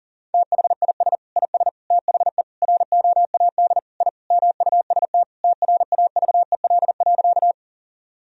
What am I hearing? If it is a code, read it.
Morse code, heard as a keyed tone, about 40 wpm, 700 Hz. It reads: THIS IS THE ROAD I MUST TRAVEL.
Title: Sentences from Top 1000 Words - Review Part 2 - 40wpm